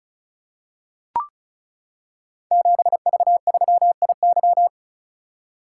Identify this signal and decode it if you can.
Morse code, heard as a keyed tone, about 36 wpm, 700 Hz. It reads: 7V3IY
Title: Code Groups - 5 Letter:Numbers - 35wpm